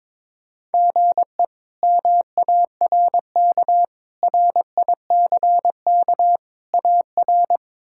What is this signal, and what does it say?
Morse code, heard as a keyed tone, about 22 wpm, 700 Hz. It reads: GE MARK RICK AR
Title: K1USN SST QSO Practice - 22wpm